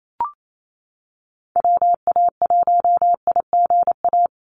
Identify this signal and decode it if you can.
Morse code, heard as a keyed tone, about 28 wpm, 700 Hz. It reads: WA1IGA
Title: USA Call Signs Repeated 3x 28wpm